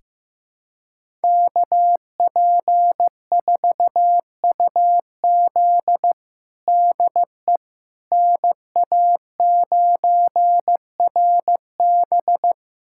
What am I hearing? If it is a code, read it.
Morse code, heard as a keyed tone, about 15 wpm, 700 Hz. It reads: KP4UZ DE NA9RB
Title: USA Callsign DE Callsign - 5 Characters - 15wpm